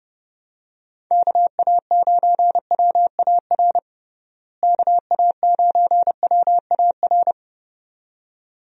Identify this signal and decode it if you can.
Morse code, heard as a keyed tone, about 30 wpm, 700 Hz. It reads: KA9WAR KA9WAR
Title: USA Call Signs - 6 Characters - Repeated 2x - 30wpm